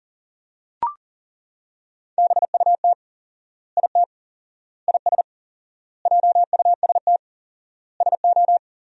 Morse code, heard as a keyed tone, about 37 wpm, 700 Hz. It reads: BUT IT IS JUST SO
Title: 4x Word-Spacing - Sentences from Top 200 Words - 40wpm